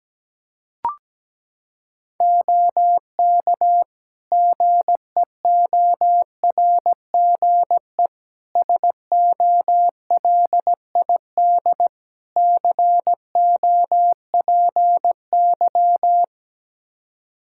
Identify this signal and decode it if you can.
Morse code, heard as a keyed tone, about 17 wpm, 700 Hz. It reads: OK GEORGE SOLID COPY
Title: K1USN SST QSO Practice 17wpm